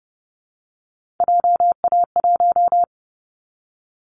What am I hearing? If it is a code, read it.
Morse code, heard as a keyed tone, about 30 wpm, 700 Hz. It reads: JA1
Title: Callsign Prefixes for Top 50 DXCC - 30wpm